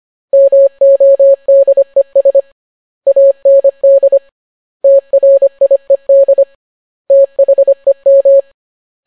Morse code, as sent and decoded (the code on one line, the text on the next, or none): -- --- -.. ... / .- -. -.. / - .-. .. -.. / - .... --
MODES AND TRIED THEM